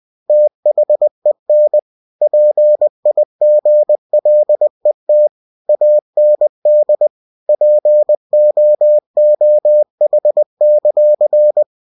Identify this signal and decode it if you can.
Morse code, heard as a keyed tone, about 20 wpm, 600 Hz. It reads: THEN PIGLET AND POOH;
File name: cw-books / winnie-the-pooh / winnie-20wpm-page-0079